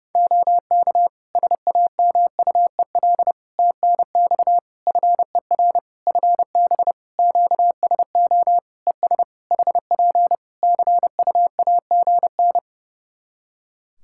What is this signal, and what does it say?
Morse code, heard as a keyed tone, about 32 wpm, 700 Hz. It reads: OK SAMUEL TNX FER FB QSO ES HP CUAGN
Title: K1USN SST QSO Practice 30wpm